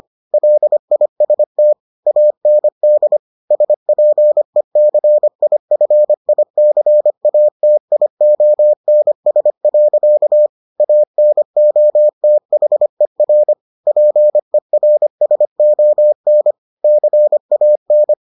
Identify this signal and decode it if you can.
Morse code, heard as a keyed tone, about 25 wpm, 600 Hz. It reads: LIST AND SPECIFICATIONS. ANOTHER PERSON CAN